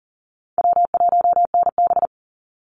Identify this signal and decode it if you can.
Morse code, heard as a keyed tone, about 40 wpm, 700 Hz. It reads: W1NB
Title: USA Call Signs 40wpm